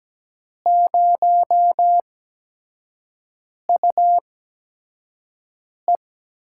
Morse code, heard as a keyed tone, about 17 wpm, spaced as for 5 wpm, 700 Hz. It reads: 0UE
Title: USA Call Signs - 4 Characters - 8x Character Spacing - 17wpm